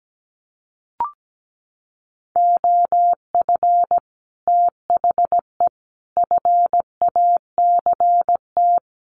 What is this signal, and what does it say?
Morse code, heard as a keyed tone, about 17 wpm, 700 Hz. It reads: OF THE FACT
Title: Sets of 5 Words Encore2 17wpm